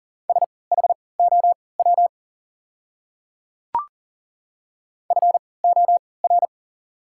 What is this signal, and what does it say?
Morse code, heard as a keyed tone, about 40 wpm, spaced as for 23 wpm, 690 Hz. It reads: SHOW FOR
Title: 3x Character Spacing - Top 200 Words - 40wpm